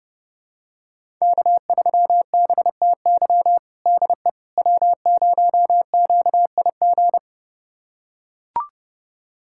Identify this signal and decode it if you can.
Morse code, heard as a keyed tone, about 30 wpm, 700 Hz. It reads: K3BTY DE W0QIG
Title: USA Callsign DE Callsign - 5 Characters - 30wpm